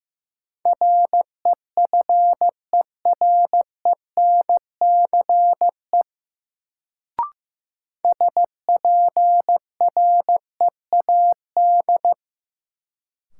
Morse code, heard as a keyed tone, about 15 wpm, 700 Hz. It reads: REFERENCE SPREAD
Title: Sentences from Top 1200 Words - New Word Drill - 15wpm